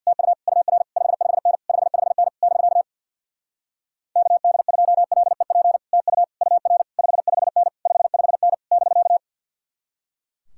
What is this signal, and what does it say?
Morse code, heard as a keyed tone, about 66 wpm, 700 Hz. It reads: TU UR 55N 55N <BK> KD2LEP TU UR 55N 55N <BK>